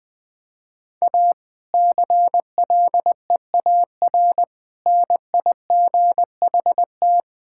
Morse code, heard as a keyed tone, about 20 wpm, 700 Hz. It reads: A CLEAR NIGHT
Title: Difficult Sentences with Hints - 20wpm